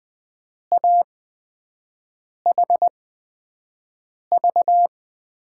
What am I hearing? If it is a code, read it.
Morse code, heard as a keyed tone, about 20 wpm, spaced as for 5 wpm, 700 Hz. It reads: AHV
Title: USA Call Signs - 6 Characters - 8x Character Spacing - 20wpm